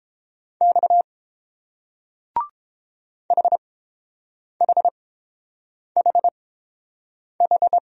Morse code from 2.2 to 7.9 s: HHHH

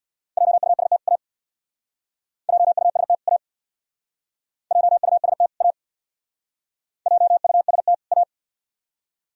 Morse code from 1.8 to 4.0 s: JUST A

